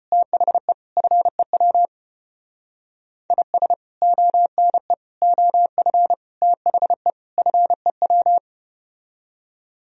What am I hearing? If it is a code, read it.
Morse code, heard as a keyed tone, about 33 wpm, 700 Hz. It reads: THE FEW IS ONE OF THE FEW